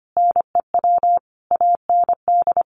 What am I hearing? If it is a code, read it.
Morse code, heard as a keyed tone, about 25 wpm, 700 Hz. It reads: NEW AND
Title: Sentences from Top 700 Words 25wpm